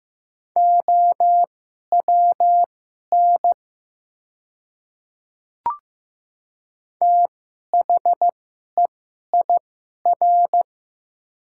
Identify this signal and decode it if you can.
Morse code, heard as a keyed tone, about 15 wpm, spaced as for 11 wpm, 700 Hz. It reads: OWN THEIR
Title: 2x Character Spacing - Top 500 Words - 15wpm